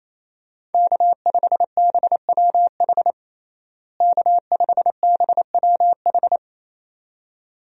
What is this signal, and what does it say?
Morse code, heard as a keyed tone, about 28 wpm, 700 Hz. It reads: K5BWH K5BWH
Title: USA Call Signs - 5 Characters - Repeated 2x - 28wpm